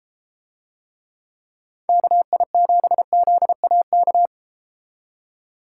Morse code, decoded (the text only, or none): KI7ZAK